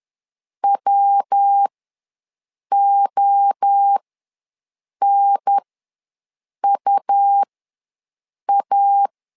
Morse code, heard as a keyed tone, about 10 wpm, spaced as for 6 wpm, 800 Hz. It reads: WONUA
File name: LEZIONI / 3 MWRO / 3 MWRO ws 10wpm 2310